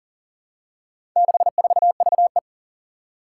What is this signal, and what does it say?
Morse code, heard as a keyed tone, about 42 wpm, 700 Hz. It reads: 64VE
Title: Four Challenging Letter-Number Combinations 40wpm